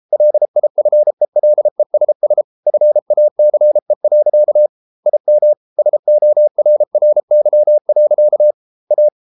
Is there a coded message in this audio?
Yes: Morse code from 0.1 to 9.1 s, heard as a keyed tone, about 33 wpm, 600 Hz. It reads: LIFELESS FACE. IM SORRY. A